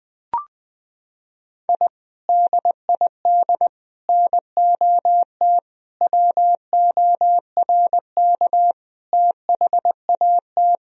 Morse code, heard as a keyed tone, about 21 wpm, 700 Hz. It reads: I DID NOT WORK THAT
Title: Sentences from Top 1000 Words - Review Part 2 - 20wpm